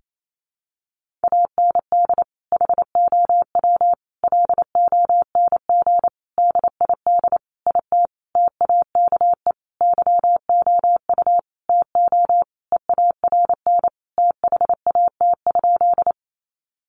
Morse code, heard as a keyed tone, about 28 wpm, 700 Hz. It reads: AND HOW LONG DID IT TAKE YOU TO EARN THAT?
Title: Sentences from Top 1500 Words 28wpm